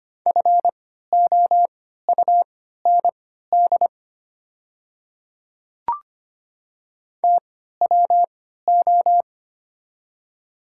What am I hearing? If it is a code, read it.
Morse code, heard as a keyed tone, about 25 wpm, spaced as for 14 wpm, 700 Hz. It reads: FOUND TWO
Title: 3x Character Spacing - Top 300 Words - 25wpm